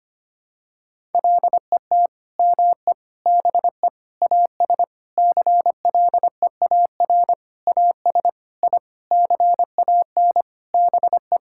Morse code, heard as a keyed tone, about 25 wpm, 700 Hz. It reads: LET ME BE AS CLEAR AS I CAN BE
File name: Sets of 9 Words - 25wpm